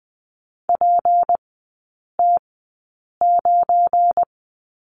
Morse code, heard as a keyed tone, about 20 wpm, spaced as for 8 wpm, 690 Hz. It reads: PT9
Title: 198 Lesson 32e - Letter k - Three Characters 20wpm